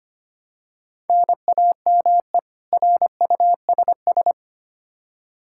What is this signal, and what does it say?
Morse code, heard as a keyed tone, about 25 wpm, 700 Hz. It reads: NAME RUSS
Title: Practice QSO - Exchange 1 - Beginning - 25wpm